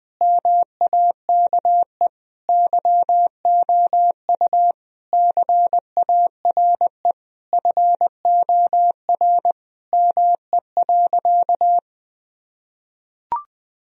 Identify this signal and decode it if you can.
Morse code, heard as a keyed tone, about 20 wpm, 700 Hz. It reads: MAKE YOU CARE FOR ME.
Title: Sentences from Top 1000 Words - Review Part 1 - 20wpm